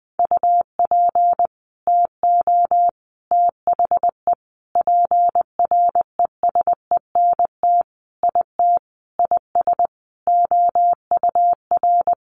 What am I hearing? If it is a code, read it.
Morse code, heard as a keyed tone, about 20 wpm, 700 Hz. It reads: UP TO THE PRESENT IT IS OUR
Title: Sentences from Top 800 Words 20wpm